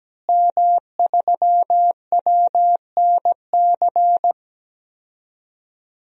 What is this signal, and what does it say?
Morse code, heard as a keyed tone, about 17 wpm, 700 Hz. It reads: M3WNC
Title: DXCC-100 Call Signs - 5 Characters - Repeated 2x - 17wpm